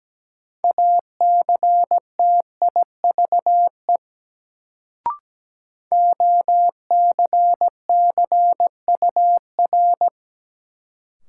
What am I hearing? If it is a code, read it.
Morse code, heard as a keyed tone, about 17 wpm, 700 Hz. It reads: ACTIVE OCCUR
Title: Sentences from Top 900 Words - New Word Drill - 17wpm